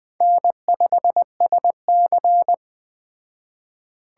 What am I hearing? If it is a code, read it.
Morse code, heard as a keyed tone, about 21 wpm, 700 Hz. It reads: N5SC